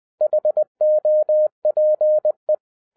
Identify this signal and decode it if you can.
Morse code, heard as a keyed tone, about 20 wpm, 610 Hz. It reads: HOPE
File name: 2012-01-mp3 / n05-CAME BACK TO HAM RADIO AFTER 30 YEARS OF INTERVAL-20wpm